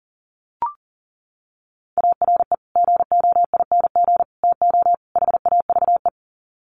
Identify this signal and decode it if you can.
Morse code, heard as a keyed tone, about 38 wpm, 700 Hz. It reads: ARE GOING TO HAVE
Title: Sets of 4 Words 40wpm